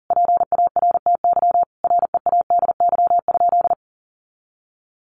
Morse code, heard as a keyed tone, about 40 wpm, 700 Hz. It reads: PARTY READY?
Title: Sentences from Top 1400 Words 40wpm